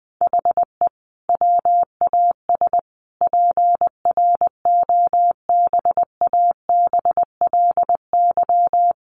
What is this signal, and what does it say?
Morse code, heard as a keyed tone, about 19 wpm, 700 Hz. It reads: HE WAS PROBABLY
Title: Sentences from Top 1300 Words 20wpm